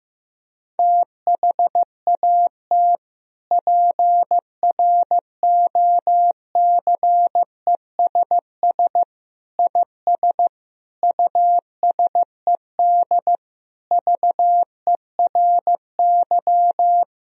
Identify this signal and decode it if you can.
Morse code, heard as a keyed tone, about 15 wpm, 700 Hz. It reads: THAT PROCESS IS USED VERY
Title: Sentences from Top 1000 Words - Review Part 3 - 15wpm